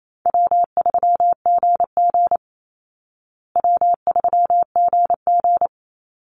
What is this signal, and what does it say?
Morse code, heard as a keyed tone, about 28 wpm, 700 Hz. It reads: W3GG W3GG
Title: Easy Call Signs Repeated 3x - 28wpm